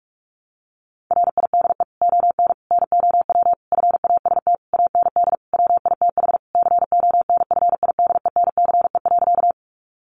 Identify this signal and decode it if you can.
Morse code, heard as a keyed tone, about 44 wpm, 700 Hz. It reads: RIDE ON NOW FAST AND WITH CONFIDENCE.
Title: Sentences from Top 1500 Words 45wpm